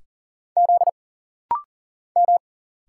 Morse code, heard as a keyed tone, about 40 wpm, 700 Hz. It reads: Z M
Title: Single Letters - Rapid-Fire 40wpm